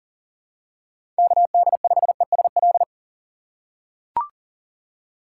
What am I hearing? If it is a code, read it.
Morse code, heard as a keyed tone, about 37 wpm, 700 Hz. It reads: KD5ESL
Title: USA Call Signs - 6 Characters - 2x Character Spacing - 40wpm